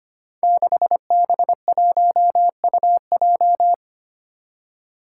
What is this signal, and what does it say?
Morse code, heard as a keyed tone, about 25 wpm, 700 Hz. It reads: 6B1UJ
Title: Five Challenging Letter-Number Combinations 25wpm